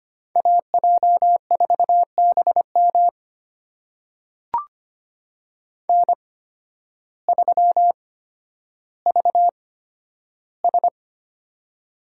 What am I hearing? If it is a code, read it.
Morse code, heard as a keyed tone, about 25 wpm, spaced as for 12 wpm, 700 Hz. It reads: AJ4BM N3VS
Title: USA Call Signs - 8x Character Spacing - 25wpm